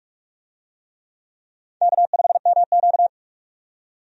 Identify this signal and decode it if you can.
Morse code, heard as a keyed tone, about 46 wpm, 700 Hz. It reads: K5MQ
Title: USA Call Signs - 4 Characters - 45wpm